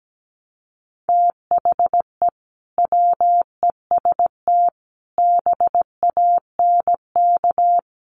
Morse code, heard as a keyed tone, about 17 wpm, 700 Hz. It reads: THE WEST BANK
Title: Sets of 5 Words 17wpm